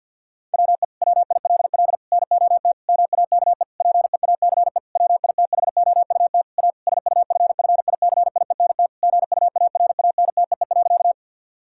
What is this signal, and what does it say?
Morse code, heard as a keyed tone, about 51 wpm, 700 Hz. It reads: WE WILL NOT MAKE PEACE WITHOUT A SUFFICIENT GUARANTEE.
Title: Sentences from Top 1600 Words 50wpm